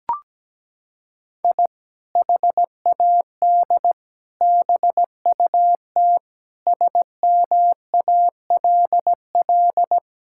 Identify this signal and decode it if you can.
Morse code, heard as a keyed tone, about 17 wpm, 700 Hz. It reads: I HAD BUT SMALL
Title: Sentences from Top 1000 Words - Review Part 6 - 17wpm